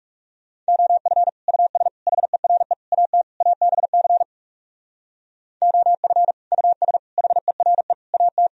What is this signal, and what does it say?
Morse code, heard as a keyed tone, about 37 wpm, 700 Hz. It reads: OF US HERE AT ABC OF US HERE AT
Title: Speed-Racing - Sets of 5 Words 30wpm